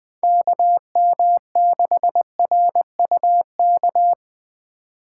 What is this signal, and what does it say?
Morse code, heard as a keyed tone, about 22 wpm, 700 Hz. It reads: KM6RUK
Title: Speed Racing - USA Call Signs 20wpm